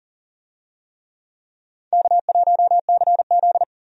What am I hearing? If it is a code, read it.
Morse code, heard as a keyed tone, about 40 wpm, 700 Hz. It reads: K1CZ
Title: USA Call Signs - 4 Characters - 40wpm